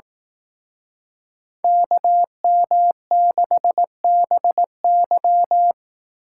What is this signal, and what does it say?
Morse code, heard as a keyed tone, about 18 wpm, 700 Hz. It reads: KM6BY
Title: Speed Racing - USA 2x2 Call Signs 15wpm